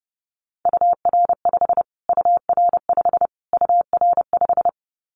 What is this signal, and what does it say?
Morse code, heard as a keyed tone, about 30 wpm, 700 Hz. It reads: UR5 UR5 UR5
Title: Callsign Prefixes for Top 25 DXCC - Repeated 3x - 30wpm